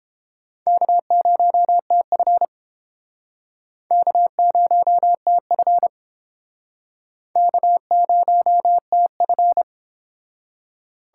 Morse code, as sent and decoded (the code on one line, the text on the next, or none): -.- ----- - ..-. / -.- ----- - ..-. / -.- ----- - ..-.
K0TF K0TF K0TF